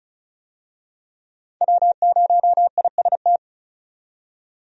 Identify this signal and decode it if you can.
Morse code, heard as a keyed tone, about 36 wpm, 700 Hz. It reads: W0IST
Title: USA Call Signs - 5 Characters - 35wpm